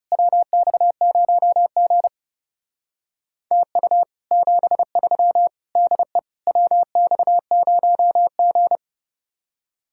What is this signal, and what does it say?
Morse code, heard as a keyed tone, about 35 wpm, 700 Hz. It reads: WX0G TU 73 DE WX0G
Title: Speed-Racing Common QSO Phrases - 30wpm